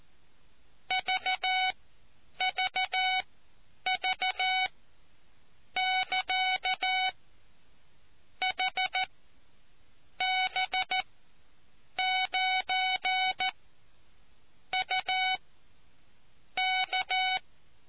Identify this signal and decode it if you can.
Morse code, heard as a keyed tone, about 14 wpm, spaced as for 7 wpm, 750 Hz. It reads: VVV <KA> HB9UK